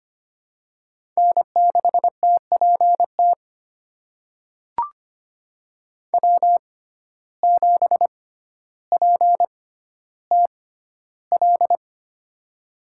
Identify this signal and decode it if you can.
Morse code, heard as a keyed tone, about 25 wpm, spaced as for 13 wpm, 700 Hz. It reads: N6TPT W7PTL
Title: USA Call Signs - 6x Character Spacing - 25wpm